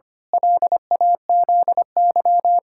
Morse code, heard as a keyed tone, about 25 wpm, 700 Hz. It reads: LAZY
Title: Pangram Challenge v2 - 25wpm